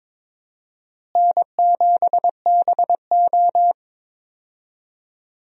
Morse code, heard as a keyed tone, about 22 wpm, 700 Hz. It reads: N7BO
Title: USA Call Signs - 4 Characters - Repeated 2x - 22wpm